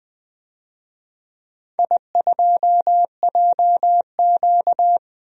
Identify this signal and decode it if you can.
Morse code, heard as a keyed tone, about 20 wpm, 700 Hz. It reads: I2JQ
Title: DXCC-100 Call Signs - 4 Characters - 20wpm